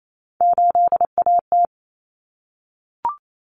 Morse code, heard as a keyed tone, about 28 wpm, 700 Hz. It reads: 8AT
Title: Common Cut Numbers - 3-Digit 28wpm